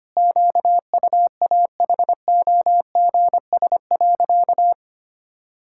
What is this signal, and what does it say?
Morse code, heard as a keyed tone, about 25 wpm, 700 Hz. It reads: QUAHOGS.
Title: Pangram Challenge v2 - 25wpm